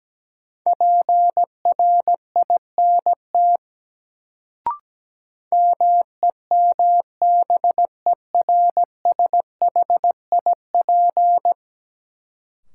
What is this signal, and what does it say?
Morse code, heard as a keyed tone, about 17 wpm, 700 Hz. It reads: PRINT MEMBERSHIP
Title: Sentences from Top 1400 Words - New Word Drill - 17wpm